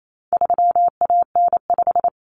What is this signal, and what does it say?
Morse code, heard as a keyed tone, about 28 wpm, 700 Hz. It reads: 3AN5
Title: Common Cut Numbers - 4-Digit 28wpm